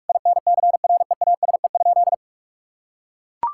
Morse code, heard as a keyed tone, about 41 wpm, 700 Hz. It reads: INCREASE?
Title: Sentences from Top 1000 Words - Review Part 5 - 45wpm